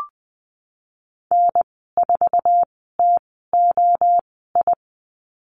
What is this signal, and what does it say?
Morse code, heard as a keyed tone, about 20 wpm, spaced as for 15 wpm, 700 Hz. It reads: N4TOI
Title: 088 Lesson 15g - Number 5 - Call Signs 20wpm